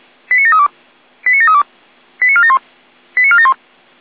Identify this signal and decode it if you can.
Five-tone selective call (ZVEI), heard as quick runs of five tones, tones of about 70 ms. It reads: 89732 89732 89471 89471
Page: SDR/ZVEI